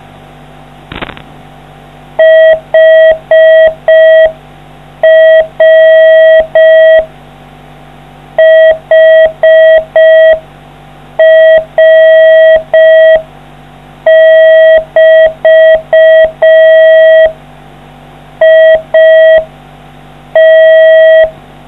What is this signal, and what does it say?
Morse code, heard as a keyed tone, about 5 wpm, 640 Hz. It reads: HR HR<BT> IT